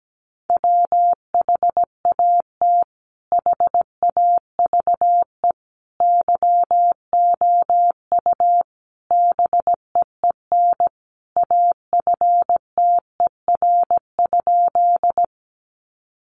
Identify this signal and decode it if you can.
Morse code, heard as a keyed tone, about 17 wpm, 700 Hz. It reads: WHAT HAVE YOU BEEN AFTER?
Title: Sentences from Top 200 Words 17wpm